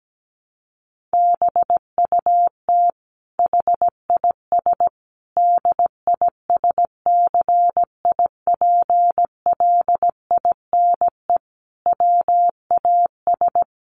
Morse code, heard as a keyed tone, about 17 wpm, 700 Hz. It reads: BUT HIS DISCIPLINE WAS
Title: Sentences from Top 1400 Words 17wpm